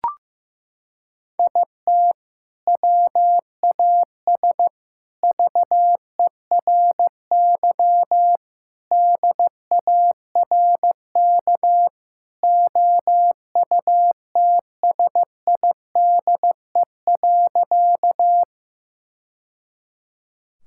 Morse code, heard as a keyed tone, about 16 wpm, 700 Hz. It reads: IT WAS VERY DARK OUTSIDE.
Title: Sentences from Top 1000 Words - Review Part 6 - 15wpm